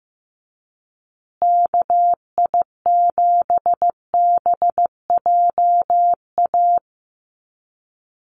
Morse code, heard as a keyed tone, about 15 wpm, 700 Hz. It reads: KI7BJA
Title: USA Call Signs Repeated 3x 15wpm